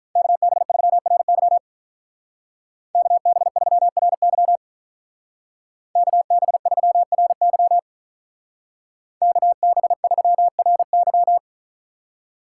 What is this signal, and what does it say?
Morse code, heard as a keyed tone, about 47 wpm, 700 Hz. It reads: KB3RY KB3RY KB3RY KB3RY